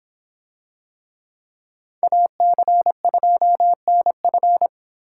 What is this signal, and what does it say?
Morse code, heard as a keyed tone, about 25 wpm, 700 Hz. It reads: AC2NF